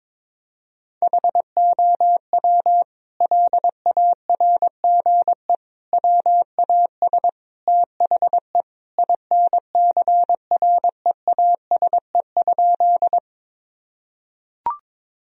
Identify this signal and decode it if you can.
Morse code, heard as a keyed tone, about 22 wpm, 700 Hz. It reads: HOW LARGE WAS THE INCREASE?
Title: Sentences from Top 1000 Words - Review Part 4 - 22wpm